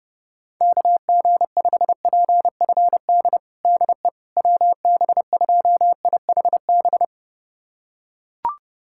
Morse code, heard as a keyed tone, about 30 wpm, 700 Hz. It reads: KG5PFD DE WB2IHB
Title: USA Callsign DE Callsign 30wpm